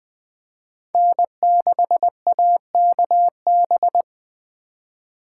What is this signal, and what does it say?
Morse code, heard as a keyed tone, about 20 wpm, 700 Hz. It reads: N6AKB
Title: Speed Racing - USA 1x3 Call Signs 17wpm